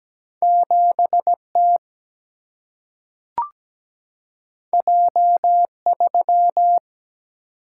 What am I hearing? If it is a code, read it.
Morse code, heard as a keyed tone, about 17 wpm, 700 Hz. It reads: 7T J3
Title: Single-Letter Single-Number 17wpm